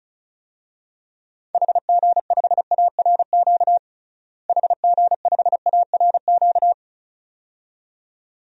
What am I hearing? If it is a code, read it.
Morse code, heard as a keyed tone, about 36 wpm, 700 Hz. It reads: HG5ARQ HG5ARQ
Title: DXCC-100 Call Signs - 6 Characters - Repeated 2x - 35wpm